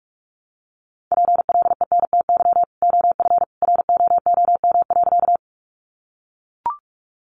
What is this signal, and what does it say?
Morse code, heard as a keyed tone, about 44 wpm, 700 Hz. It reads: PLENTY OF ROOM.
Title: Sentences from Top 1300 Words 45wpm